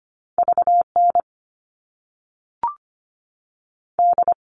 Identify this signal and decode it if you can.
Morse code, heard as a keyed tone, about 25 wpm, 700 Hz. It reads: V N D
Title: All Cut Numbers - 2-Digit 25wpm